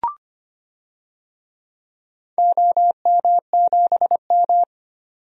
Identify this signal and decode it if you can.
Morse code, heard as a keyed tone, about 25 wpm, 700 Hz. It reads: OM7M
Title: DXCC-100 Call Signs - 4 Characters - 25wpm